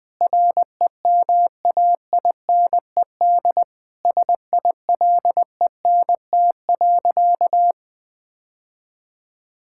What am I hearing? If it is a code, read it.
Morse code, heard as a keyed tone, about 20 wpm, 700 Hz. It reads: REMAINED SILENT.
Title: Sentences with 7-Word Phrases - Spoken Beginning - 20wpm